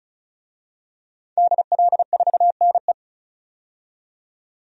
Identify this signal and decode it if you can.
Morse code, heard as a keyed tone, about 36 wpm, 700 Hz. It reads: DL4NE
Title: DXCC-100 Call Signs - 5 Characters - 35wpm